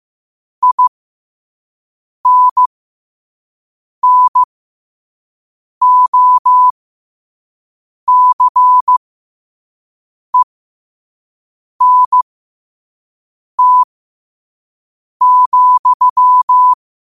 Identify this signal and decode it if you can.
Morse code, heard as a keyed tone, about 15 wpm, spaced as for 5 wpm, 1000 Hz. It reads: INNOCENT,
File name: qotdfiles / 5wpm